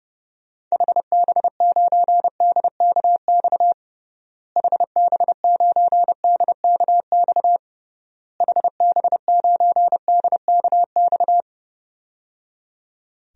Morse code, heard as a keyed tone, about 30 wpm, 700 Hz. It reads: HB9DKX HB9DKX HB9DKX
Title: DXCC-100 Call Signs - 6 Characters - Repeated 3x - 30wpm